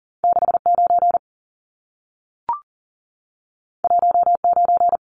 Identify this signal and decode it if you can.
Morse code, heard as a keyed tone, about 40 wpm, 700 Hz. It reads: 69 19
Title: Numbers - 2-Digit 40wpm